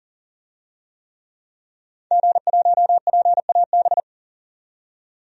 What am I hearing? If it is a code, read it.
Morse code, heard as a keyed tone, about 41 wpm, 700 Hz. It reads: G1PAB
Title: DXCC-100 Call Signs - 5 Characters - 40wpm